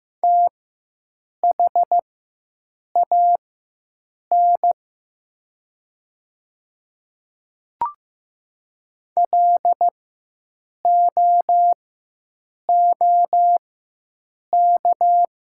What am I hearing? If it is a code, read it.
Morse code, heard as a keyed tone, about 15 wpm, spaced as for 7 wpm, 700 Hz. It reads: THAN LOOK
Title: 4x Character Spacing - Top 100 Words - 15wpm